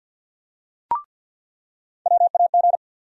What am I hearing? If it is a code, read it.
Morse code, heard as a keyed tone, about 53 wpm, 700 Hz. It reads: WAG
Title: USA Call Sign Postfixes - 3 Letters 50wpm